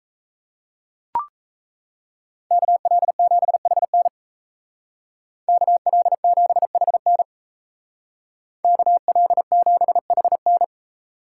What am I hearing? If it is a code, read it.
Morse code, heard as a keyed tone, about 34 wpm, 700 Hz. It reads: KL7HN KL7HN KL7HN
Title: Speed Racing - USA 2x2 Call Signs 28wpm